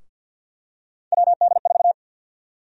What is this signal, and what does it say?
Morse code, heard as a keyed tone, about 53 wpm, 700 Hz. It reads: WD4
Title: USA Call Sign Prefixes - Repeated 2x - 50wpm